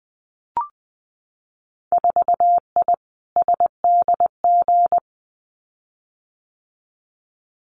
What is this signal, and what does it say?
Morse code, heard as a keyed tone, about 21 wpm, 700 Hz. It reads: 4I SDG
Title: Winter Field Day Exchanges Encore 20wpm